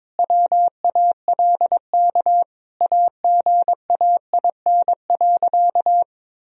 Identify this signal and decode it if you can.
Morse code, heard as a keyed tone, about 22 wpm, 700 Hz. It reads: WALK AGAIN.
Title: Sentences from Top 1000 Words - Review Part 1 - 22wpm